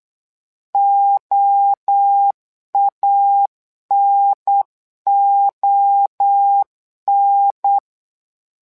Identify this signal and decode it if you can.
Morse code, heard as a keyed tone, about 8 wpm, 800 Hz. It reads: OANON
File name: LEZIONI / 3 MWRO / 3 MWRO 8wpm 2606